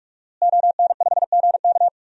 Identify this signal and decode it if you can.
Morse code, heard as a keyed tone, about 44 wpm, 700 Hz. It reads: ON5GK
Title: DXCC-100 Call Signs - 5 Characters - Repeated 3x - 45wpm